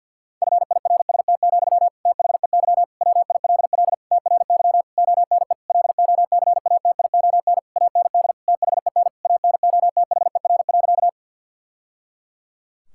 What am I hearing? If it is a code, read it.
Morse code, heard as a keyed tone, about 50 wpm, 700 Hz. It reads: FIRST, THEY WILL TRY ONE LOCATION AND THEN ANOTHER.
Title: Sentences from Top 1000 Words - Review Part 6 - 50wpm